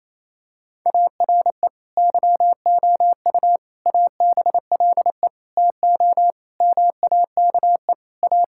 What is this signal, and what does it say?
Morse code, heard as a keyed tone, about 27 wpm, 700 Hz. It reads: ARE YOU ABLE TO MAKE A
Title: Sentences from Top 1000 Words - Review Part 6 - 28wpm